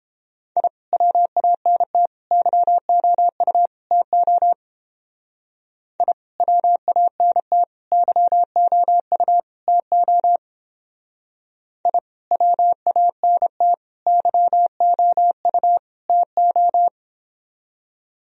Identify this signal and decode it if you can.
Morse code, heard as a keyed tone, about 31 wpm, 700 Hz. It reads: I WANT YOU TO I WANT YOU TO I WANT YOU TO